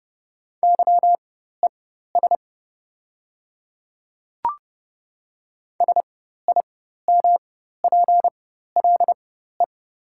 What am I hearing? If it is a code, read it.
Morse code, heard as a keyed tone, about 30 wpm, spaced as for 14 wpm, 700 Hz. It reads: YES SIMPLE
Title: 4x Character Spacing - Top 500 Words - 30wpm